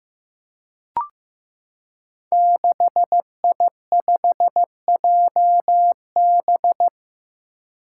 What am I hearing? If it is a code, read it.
Morse code, heard as a keyed tone, about 15 wpm, 700 Hz. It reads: 6I5JB
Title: Five Challenging Letter-Number Combinations 15wpm